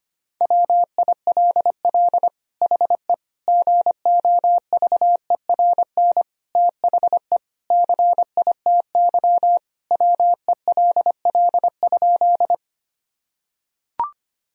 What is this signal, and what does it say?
Morse code, heard as a keyed tone, about 25 wpm, 700 Hz. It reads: WILL HE GOVERN THE CITY WELL?
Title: Sentences from Top 1000 Words - Review Part 2 - 25wpm